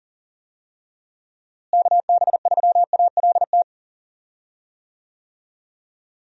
Morse code, heard as a keyed tone, about 39 wpm, 700 Hz. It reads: KB3ALT
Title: USA Call Signs - 6 Characters - Repeated 3x - 40wpm